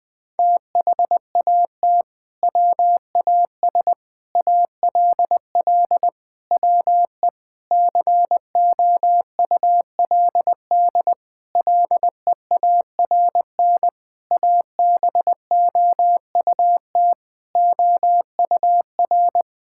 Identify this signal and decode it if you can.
Morse code, heard as a keyed tone, about 20 wpm, 700 Hz. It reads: THAT WAS ALL WE COULD LEARN ABOUT OUR
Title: Sentences from Top 1600 Words 20wpm